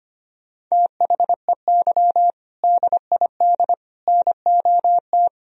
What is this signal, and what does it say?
Morse code, heard as a keyed tone, about 25 wpm, 700 Hz. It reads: THEY DID NOT
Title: Sentences from Top 1000 Words - Review Part 2 - 25wpm